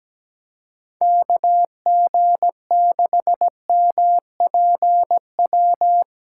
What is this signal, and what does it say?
Morse code, heard as a keyed tone, about 17 wpm, 700 Hz. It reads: KG6MPW
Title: USA Callsign DE Callsign - 6 Characters - 17wpm